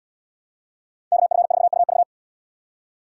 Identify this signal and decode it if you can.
Morse code, heard as a keyed tone, about 72 wpm, 700 Hz. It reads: KF4UV